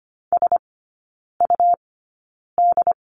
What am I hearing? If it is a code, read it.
Morse code, heard as a keyed tone, about 25 wpm, spaced as for 9 wpm, 700 Hz. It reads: SUD
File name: Lesson 17e - Letter u - Three Characters 25wpm